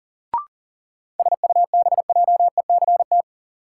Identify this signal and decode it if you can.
Morse code, heard as a keyed tone, about 37 wpm, 700 Hz. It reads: SUBJECT
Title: Sentences from Top 800 Words - New Word Drill - 40wpm